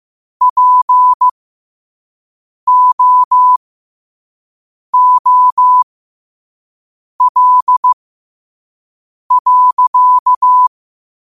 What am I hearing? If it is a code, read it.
Morse code, heard as a keyed tone, about 15 wpm, spaced as for 5 wpm, 1000 Hz. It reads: POOL.